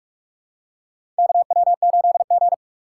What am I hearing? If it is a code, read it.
Morse code, heard as a keyed tone, about 41 wpm, 700 Hz. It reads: KW8G
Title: USA Call Signs - 4 Characters - 3x Character Spacing - 45wpm